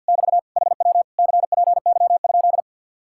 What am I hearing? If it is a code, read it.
Morse code, heard as a keyed tone, about 58 wpm, 700 Hz. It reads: <BT> HW CPY?
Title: Speed-Racing Common QSO Phrases Encore - 50wpm